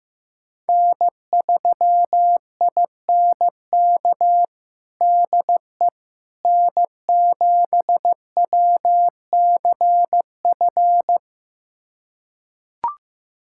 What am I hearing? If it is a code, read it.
Morse code, heard as a keyed tone, about 15 wpm, 700 Hz. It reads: N3INK DE N7WCF
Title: USA Callsign DE Callsign - 5 Characters - 15wpm